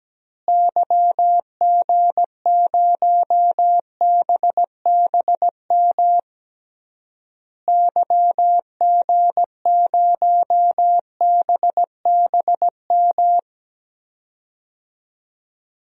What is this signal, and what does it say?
Morse code, heard as a keyed tone, about 17 wpm, 700 Hz. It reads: YG0BBM YG0BBM
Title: DXCC-100 Call Signs - 6 Characters - Repeated 3x - 17wpm